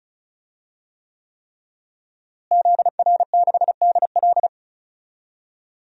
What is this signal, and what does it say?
Morse code, heard as a keyed tone, about 36 wpm, 700 Hz. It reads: ZR6DL
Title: DXCC-100 Call Signs - 5 Characters - Repeated 3x - 35wpm